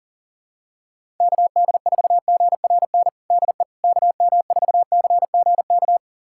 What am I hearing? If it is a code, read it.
Morse code, heard as a keyed tone, about 40 wpm, 700 Hz. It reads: KD4GRN DE KM4CGK
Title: USA Callsign DE Callsign - 6 Characters - 40wpm